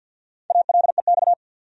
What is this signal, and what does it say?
Morse code, heard as a keyed tone, about 48 wpm, 700 Hz. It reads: ALEX
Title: USA Common Names - 50wpm